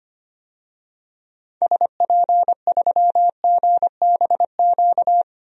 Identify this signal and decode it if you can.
Morse code, heard as a keyed tone, about 25 wpm, 700 Hz. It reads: SP3GBQ
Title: DXCC-100 Call Signs - 6 Characters - 25wpm